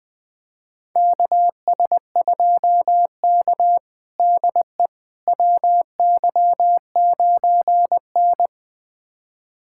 Morse code, heard as a keyed tone, about 21 wpm, 700 Hz. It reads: KS2K DE WY9N
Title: USA Callsign DE Callsign - 4 Characters - 20wpm